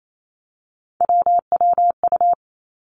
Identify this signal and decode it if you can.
Morse code, heard as a keyed tone, about 28 wpm, 700 Hz. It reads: WWU
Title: Three Challenging Letter-Number Combinations 28wpm